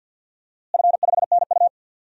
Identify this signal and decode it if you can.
Morse code, heard as a keyed tone, about 52 wpm, 700 Hz. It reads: F5NU
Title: DXCC-100 Call Signs - 4 Characters - Repeated 2x - 50wpm